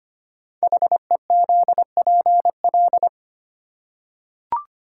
Morse code, heard as a keyed tone, about 25 wpm, 700 Hz. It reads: HEZPL
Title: Code Groups - 5 Letters - 25wpm